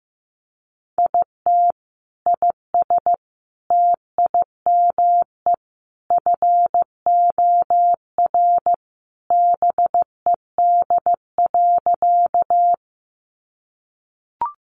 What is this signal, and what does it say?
Morse code, heard as a keyed tone, about 15 wpm, 700 Hz. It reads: IT IS TIME FOR BED.
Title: Sentences from Top 500 Words 15wpm